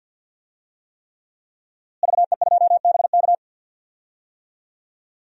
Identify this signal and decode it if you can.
Morse code, heard as a keyed tone, about 48 wpm, 700 Hz. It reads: VE2BK
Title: DXCC-100 Call Signs - 5 Characters - Repeated 3x - 50wpm